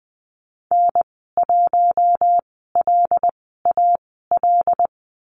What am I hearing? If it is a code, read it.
Morse code, heard as a keyed tone, about 20 wpm, spaced as for 15 wpm, 700 Hz. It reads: N1LAL
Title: 140 Lesson 23g - Letter f - Call Signs 20wpm